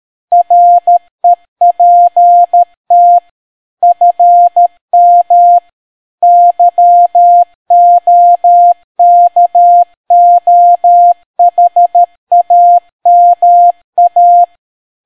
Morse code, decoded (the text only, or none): REPT FM YOKOHAMA